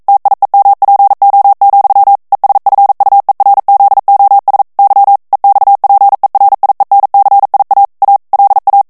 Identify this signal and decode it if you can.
Morse code, heard as a keyed tone, about 42 wpm, 800 Hz. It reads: TIEMPO, ESFUERZOS Y EXPERIENCIA A LA
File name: morse mp3 / ESPAGNOL TEXTE EN CLAIR 1 VITESSE 40 MOTS MINUTES